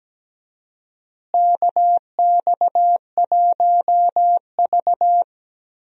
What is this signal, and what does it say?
Morse code, heard as a keyed tone, about 17 wpm, 700 Hz. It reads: KX1V
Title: USA Call Signs - 4 Characters - Repeated 3x - 17wpm